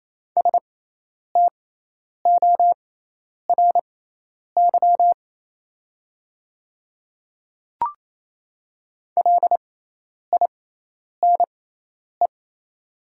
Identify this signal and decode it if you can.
Morse code, heard as a keyed tone, about 28 wpm, spaced as for 10 wpm, 700 Hz. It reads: STORY LINE
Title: 6x Character Spacing - Top 300 Words - 28wpm